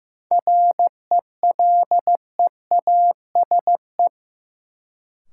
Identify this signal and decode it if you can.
Morse code, heard as a keyed tone, about 15 wpm, 700 Hz. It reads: RELEASE
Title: Sentences from Top 1100 Words - New Word Drill Alt - 15wpm